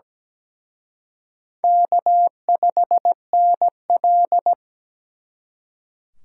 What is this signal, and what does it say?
Morse code, heard as a keyed tone, about 20 wpm, 700 Hz. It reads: K5NL
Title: Speed Racing - USA 1x2 Call Signs 17wpm